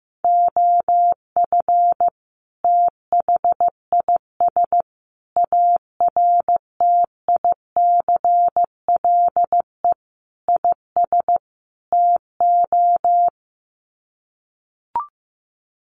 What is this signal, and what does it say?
Morse code, heard as a keyed tone, about 15 wpm, 700 Hz. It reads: OF THIS ARTICLE IS TO
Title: Sets of 5 Words Encore2 15wpm